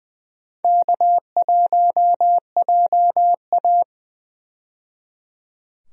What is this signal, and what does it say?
Morse code, heard as a keyed tone, about 20 wpm, 700 Hz. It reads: K1JA
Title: USA Call Signs - 4 Characters - 20wpm